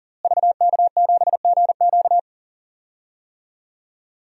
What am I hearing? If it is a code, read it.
Morse code, heard as a keyed tone, about 37 wpm, 700 Hz. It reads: VK7GQ
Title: DXCC-100 Call Signs - 5 Characters - Repeated 2x - 40wpm